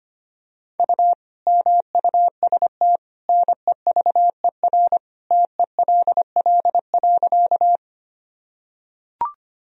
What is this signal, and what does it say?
Morse code, heard as a keyed tone, about 25 wpm, 700 Hz. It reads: U MUST NEVER TELL.
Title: Sentences from Top 300 Words and Abbreviations 25wpm